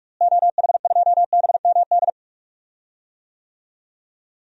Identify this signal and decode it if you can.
Morse code, heard as a keyed tone, about 41 wpm, 700 Hz. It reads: OH2BMD
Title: DXCC-100 Call Signs - 6 Characters - Repeated 3x - 45wpm